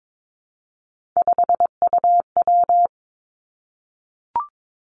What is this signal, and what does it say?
Morse code, heard as a keyed tone, about 22 wpm, 700 Hz. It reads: 5UW
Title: Three Challenging Letter-Number Combinations 22wpm